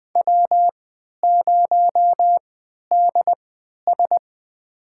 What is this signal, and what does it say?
Morse code, heard as a keyed tone, about 20 wpm, spaced as for 11 wpm, 700 Hz. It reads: W0DS
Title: USA Call Signs - 4 Characters - 3x Character Spacing - 20wpm